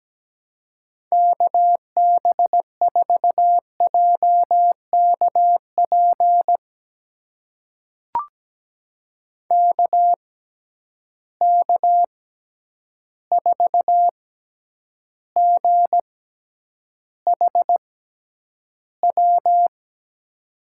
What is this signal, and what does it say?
Morse code, heard as a keyed tone, about 17 wpm, spaced as for 9 wpm, 700 Hz. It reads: KB4JKP KK4GHW
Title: USA Call Signs - 6 Characters - 6x Character Spacing - 17wpm